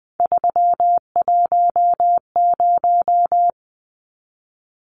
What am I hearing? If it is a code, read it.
Morse code, heard as a keyed tone, about 20 wpm, 700 Hz. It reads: 310
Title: Numbers - 3-Digit 20wpm